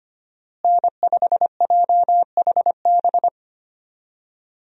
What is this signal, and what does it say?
Morse code, heard as a keyed tone, about 25 wpm, 700 Hz. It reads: N5JHB
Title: USA Call Signs - 5 Characters - Repeated 3x - 25wpm